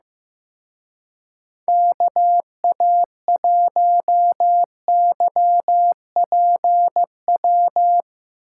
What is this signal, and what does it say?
Morse code, heard as a keyed tone, about 15 wpm, 700 Hz. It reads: KA1YPW
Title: Speed Racing - USA 2x3 Call Signs 15wpm